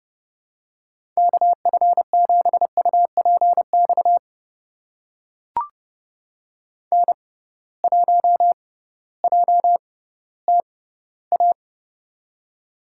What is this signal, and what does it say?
Morse code, heard as a keyed tone, about 30 wpm, spaced as for 16 wpm, 700 Hz. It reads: KF7UPX N1JTA
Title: USA Call Signs - 6x Character Spacing - 30wpm